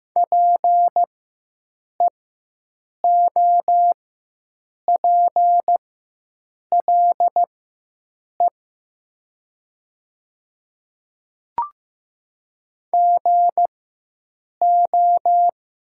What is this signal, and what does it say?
Morse code, heard as a keyed tone, about 15 wpm, spaced as for 7 wpm, 700 Hz. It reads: PEOPLE GO
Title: 4x Character Spacing - Top 500 Words - 15wpm